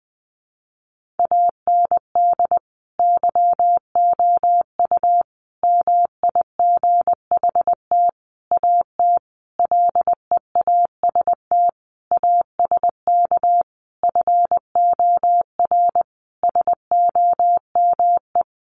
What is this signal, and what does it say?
Morse code, heard as a keyed tone, about 20 wpm, 700 Hz. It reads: AND YOU MIGHT AT LEAST ASK FOR SOME
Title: Sentences from Top 1500 Words 20wpm